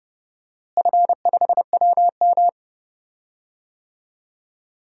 Morse code, heard as a keyed tone, about 30 wpm, 700 Hz. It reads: F5WM
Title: DXCC-100 Call Signs - 4 Characters - Repeated 3x - 30wpm